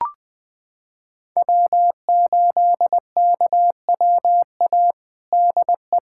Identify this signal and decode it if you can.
Morse code, heard as a keyed tone, about 21 wpm, 700 Hz. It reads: W8KWA DE
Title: USA Callsign DE Callsign - 5 Characters - 20wpm